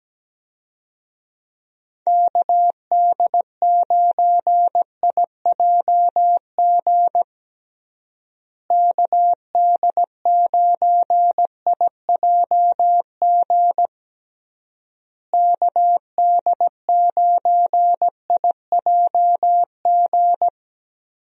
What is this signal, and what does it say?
Morse code, heard as a keyed tone, about 17 wpm, 700 Hz. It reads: KD9IJG KD9IJG KD9IJG
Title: USA Call Signs - 6 Characters - Repeated 3x - 17wpm